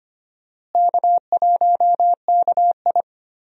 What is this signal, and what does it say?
Morse code, heard as a keyed tone, about 25 wpm, 700 Hz. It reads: K1KI
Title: USA Call Signs - 4 Characters - Repeated 2x - 25wpm